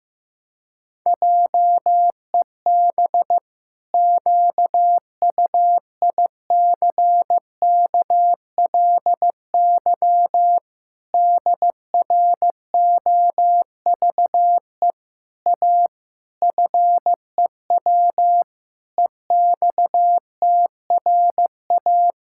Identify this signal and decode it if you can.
Morse code, heard as a keyed tone, about 16 wpm, 700 Hz. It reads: JEB QUICKLY DROVE A FEW EXTRA
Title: Pangram Challenge v2 - 15wpm